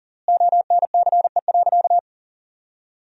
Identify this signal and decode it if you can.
Morse code, heard as a keyed tone, about 40 wpm, 700 Hz. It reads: ONCE.
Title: 4x Word-Spacing - Sentences from Top 500 Words - 40wpm